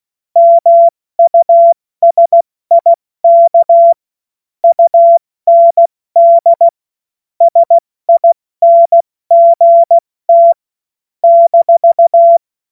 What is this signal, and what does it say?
Morse code, heard as a keyed tone, about 16 wpm, 680 Hz. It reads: MUSIK UND SINGT -
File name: Morsekurs / Witze / Witz 10. August 2021, 70 F80